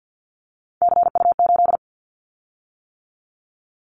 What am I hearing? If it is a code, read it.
Morse code, heard as a keyed tone, about 50 wpm, 700 Hz. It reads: CU8
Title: Callsign Prefixes for Top 50 DXCC - 50wpm